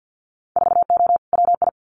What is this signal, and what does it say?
Morse code, heard as a keyed tone, about 50 wpm, 700 Hz. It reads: 4O WI
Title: Winter Field Day Exchanges 50wpm